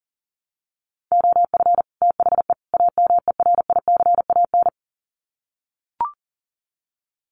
Morse code, heard as a keyed tone, about 37 wpm, 700 Hz. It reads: OF THE AMERICAN
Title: Sets of 3 Words 40wpm